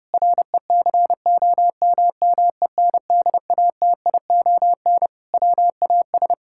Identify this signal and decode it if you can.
Morse code, heard as a keyed tone, about 32 wpm, 700 Hz. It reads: RECOMMENDATION WAS
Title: Sentences from Top 1600 Words 30wpm